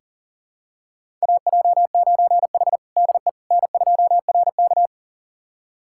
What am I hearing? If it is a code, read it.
Morse code, heard as a keyed tone, about 40 wpm, 700 Hz. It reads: AJ9H DE N2RK
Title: USA Callsign DE Callsign - 4 Characters - 40wpm